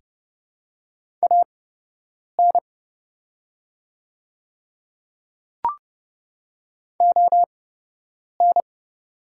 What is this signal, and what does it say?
Morse code, heard as a keyed tone, about 30 wpm, spaced as for 8 wpm, 700 Hz. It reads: AN ON
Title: 8x Character Spacing - Top 500 Words - 30wpm